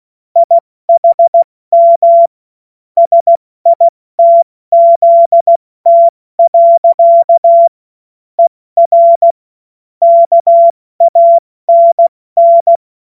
Morse code, heard as a keyed tone, about 16 wpm, 680 Hz. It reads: IHM SITZT. ER KANN
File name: Morsekurs / Witze / Witz 20. September 2022 70 F80